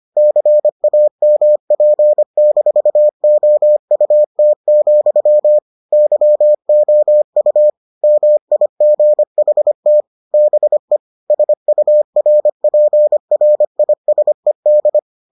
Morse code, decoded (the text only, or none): CAMP-OUT, YOU MIGHT BE SURPRISED